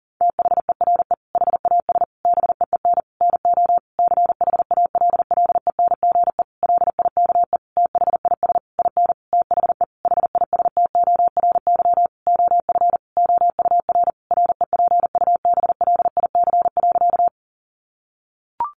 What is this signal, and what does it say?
Morse code, heard as a keyed tone, about 40 wpm, 700 Hz. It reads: THERE HAS BEEN NO CHALLENGE LIKE THIS IN THE HISTORY OF OUR REPUBLIC.
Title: Sentences from Top 1500 Words 40wpm